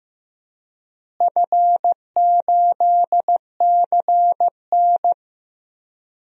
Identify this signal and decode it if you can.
Morse code, heard as a keyed tone, about 16 wpm, 700 Hz. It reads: F8CN
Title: DXCC-100 Call Signs - 4 Characters - Repeated 3x - 15wpm